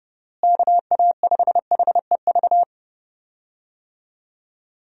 Morse code, heard as a keyed tone, about 30 wpm, 700 Hz. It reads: KA5HEV
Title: USA Call Signs - 6 Characters - Repeated 3x - 30wpm